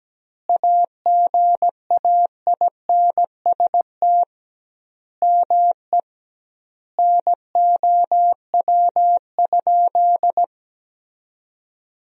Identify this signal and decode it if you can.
Morse code, heard as a keyed tone, about 17 wpm, 700 Hz. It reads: AGAINST ME NOW?
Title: 2x Word-Spacing - Sentences from Top 400 Words - 17wpm